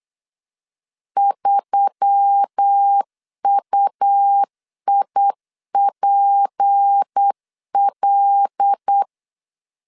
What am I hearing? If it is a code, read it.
Morse code, heard as a keyed tone, about 8 wpm, 800 Hz. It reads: 3UIPL
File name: LEZIONI / 6 PKX0 / 6 P 8wpm 3025